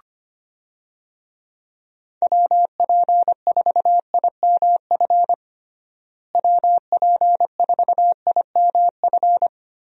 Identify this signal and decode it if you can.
Morse code, heard as a keyed tone, about 25 wpm, 700 Hz. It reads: WP4IMF WP4IMF
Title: USA Call Signs - 6 Characters - Repeated 2x - 25wpm